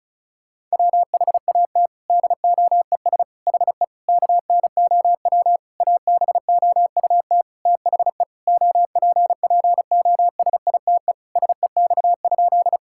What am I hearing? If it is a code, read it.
Morse code, heard as a keyed tone, about 36 wpm, 700 Hz. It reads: WHAT DOES HE KNOW ABOUT THE OPPOSITE SEX?
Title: Sentences from Top 1600 Words 35wpm